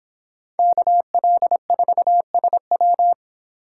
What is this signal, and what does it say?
Morse code, heard as a keyed tone, about 30 wpm, 700 Hz. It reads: KL4SW